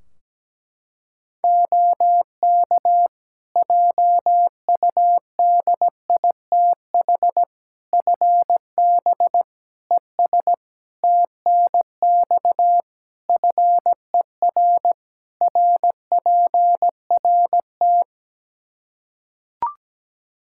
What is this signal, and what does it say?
Morse code, heard as a keyed tone, about 17 wpm, 700 Hz. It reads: OK JUDITH FB ES TNX FER RPRT
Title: Practice QSO - Exchange 2 - Info - 17wpm